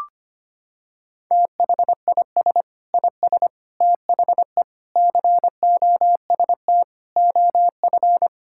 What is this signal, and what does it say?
Morse code, heard as a keyed tone, about 25 wpm, 700 Hz. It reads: THIS IS THE COST OF
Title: Sentences from Top 1000 Words - Review Part 5 - 25wpm